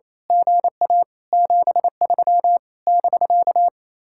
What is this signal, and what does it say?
Morse code, heard as a keyed tone, about 27 wpm, 700 Hz. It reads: GA 73 <BK>
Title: Speed-Racing Common QSO Phrases Encore - 28wpm